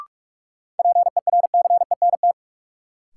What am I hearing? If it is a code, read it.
Morse code, heard as a keyed tone, about 46 wpm, 700 Hz. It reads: PERCENT
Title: Sets of 2 Words - New Word Drill - 45wpm